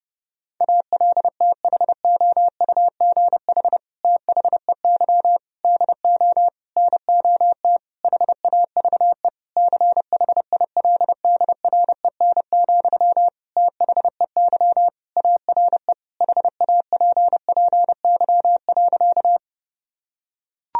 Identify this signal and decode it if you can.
Morse code, heard as a keyed tone, about 30 wpm, 700 Hz. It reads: ALTHOUGH THEY DO NOT HAVE CHILDREN, THEY ARE HAPPY.
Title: Sentences from Top 1000 Words - Review Part 3 - 30wpm